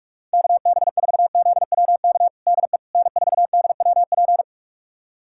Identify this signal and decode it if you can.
Morse code, heard as a keyed tone, about 42 wpm, 700 Hz. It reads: KB4ZWK DE N4DWP
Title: USA Callsign DE Callsign 45wpm